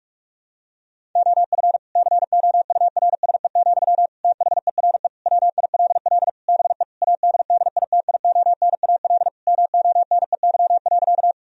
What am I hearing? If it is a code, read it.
Morse code, heard as a keyed tone, about 46 wpm, 700 Hz. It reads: OF COURSE, THERE WILL BE ADDITIONAL MONEY.
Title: Sentences from Top 1000 Words - Review Part 4 - 45wpm